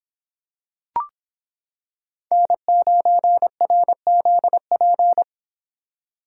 Morse code, heard as a keyed tone, about 26 wpm, 700 Hz. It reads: N9RZP